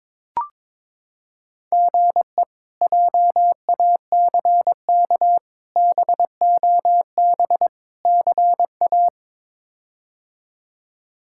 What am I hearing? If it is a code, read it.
Morse code, heard as a keyed tone, about 22 wpm, 700 Hz. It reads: GE JACK BOB CA
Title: K1USN SST QSO Practice - 22wpm